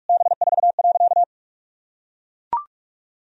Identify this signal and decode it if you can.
Morse code, heard as a keyed tone, about 41 wpm, 700 Hz. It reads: B4.
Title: Sentences from Top 400 Words and Abbreviations 45wpm